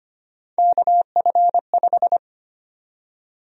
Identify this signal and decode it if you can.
Morse code, heard as a keyed tone, about 25 wpm, 700 Hz. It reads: KF5
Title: USA Call Sign Prefixes - 25wpm